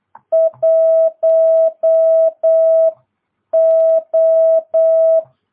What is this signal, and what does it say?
Morse code, heard as a keyed tone, about 8 wpm, spaced as for 7 wpm, 650 Hz. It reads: 1O